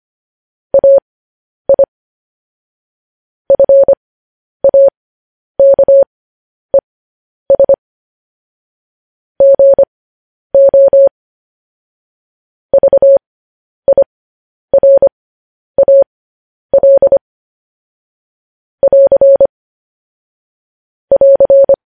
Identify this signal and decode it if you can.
Morse code, heard as a keyed tone, about 25 wpm, spaced as for 10 wpm, 550 Hz. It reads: AI FAKES GO VIRAL <AR> <AR>